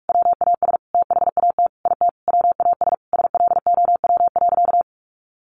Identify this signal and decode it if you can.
Morse code, heard as a keyed tone, about 44 wpm, 700 Hz. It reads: WAS THAT IT WAS SLOW.
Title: Sentences from Top 1400 Words 45wpm